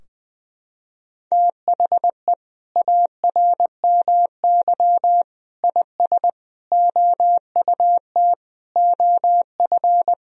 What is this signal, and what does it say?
Morse code, heard as a keyed tone, about 21 wpm, 700 Hz. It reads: THE ARMY IS OUT OF
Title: Sentences from Top 1000 Words - Review Part 4 - 20wpm